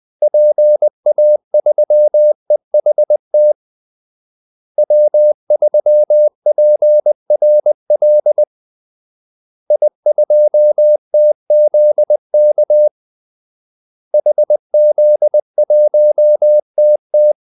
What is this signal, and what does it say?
Morse code, heard as a keyed tone, about 20 wpm, 600 Hz. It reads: PA3EHT W3PRL I2TZK HZ1TT
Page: Kaldesignaler 20 wpm